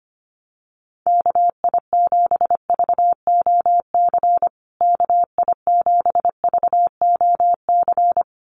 Morse code, heard as a keyed tone, about 25 wpm, 700 Hz. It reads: KI7VOC KI7VOC
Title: USA Call Signs Repeated 2x 25wpm